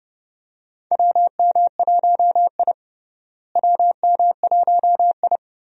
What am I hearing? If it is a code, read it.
Morse code, heard as a keyed tone, about 29 wpm, 700 Hz. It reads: WM1I WM1I
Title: USA Call Signs - 4 Characters - Repeated 3x - 30wpm